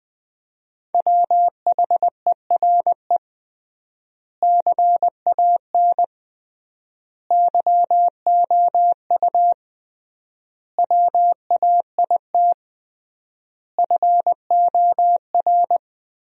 Morse code, heard as a keyed tone, about 20 wpm, 700 Hz. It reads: WHERE CAN YOU WAIT FOR
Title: 3x Word-Spacing - Sentences from Top 500 Words - 20wpm